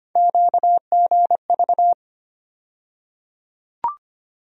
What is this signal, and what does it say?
Morse code, heard as a keyed tone, about 25 wpm, 700 Hz. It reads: QGV
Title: USA Call Sign Postfixes - 3 Letters 25wpm